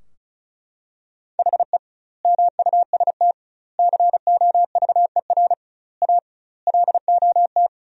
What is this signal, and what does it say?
Morse code, heard as a keyed tone, about 36 wpm, 700 Hz. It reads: HE MUST COVER A LOT
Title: 2x Word-Spacing - Sentences from Top 500 Words - 35wpm